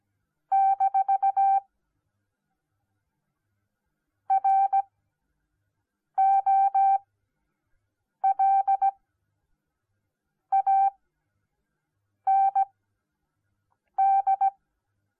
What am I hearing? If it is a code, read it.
Morse code, heard as a keyed tone, about 17 wpm, spaced as for 5 wpm, 780 Hz. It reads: - ROLAND